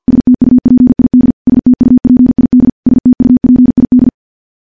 Morse code, heard as a keyed tone, about 50 wpm, 250 Hz. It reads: STUPID STUPID STUPID
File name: morse / morse-training-50WPM